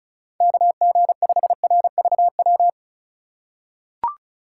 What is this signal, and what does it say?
Morse code, heard as a keyed tone, about 34 wpm, 700 Hz. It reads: KG5RVW
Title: USA Call Signs - 4x Character Spacing - 35wpm